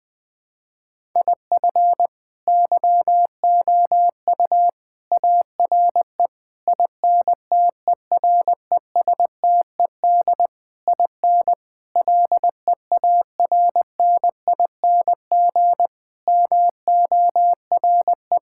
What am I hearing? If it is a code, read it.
Morse code, heard as a keyed tone, about 20 wpm, 700 Hz. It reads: IF YOU ARE INTERESTED IN LEARNING MORE
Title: Sets of 8 Words - 20wpm